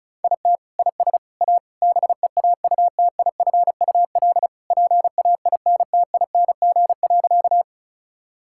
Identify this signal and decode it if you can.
Morse code, heard as a keyed tone, about 34 wpm, 690 Hz. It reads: IT IS A BEAUTIFUL PAINTING.
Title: Sentences from Top 1000 Words - Review Part 6 - 35wpm